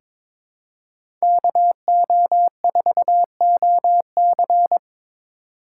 Morse code, heard as a keyed tone, about 22 wpm, 700 Hz. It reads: KO4OC
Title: USA Call Signs - 5 Characters - 8x Character Spacing - 22wpm